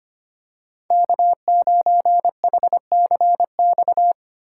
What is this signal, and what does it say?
Morse code, heard as a keyed tone, about 25 wpm, 700 Hz. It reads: K9HCX
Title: USA Call Signs - 3x Character Spacing - 25wpm